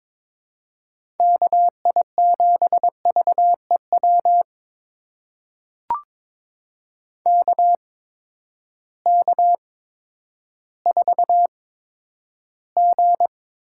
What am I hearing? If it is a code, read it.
Morse code, heard as a keyed tone, about 22 wpm, spaced as for 11 wpm, 700 Hz. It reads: KI7VEW KK4G
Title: USA Call Signs - 6 Characters - 8x Character Spacing - 22wpm